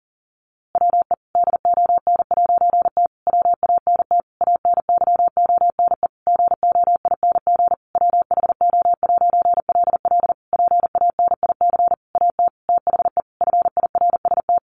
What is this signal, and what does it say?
Morse code, heard as a keyed tone, about 40 wpm, 700 Hz. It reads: WE DON'T WANT ANYONE GOING WHO'LL PANIC AT THE FIRST
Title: A Sound Of Thunder 40wpm